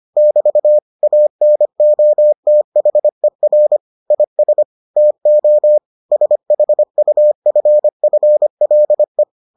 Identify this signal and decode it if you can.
Morse code, heard as a keyed tone, about 25 wpm, 600 Hz. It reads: <BT> ANOTHER IS TO SHUFFLE